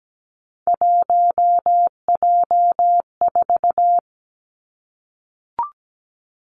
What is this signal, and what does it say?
Morse code, heard as a keyed tone, about 17 wpm, 700 Hz. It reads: 1J4
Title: Three Challenging Letter-Number Combinations 17wpm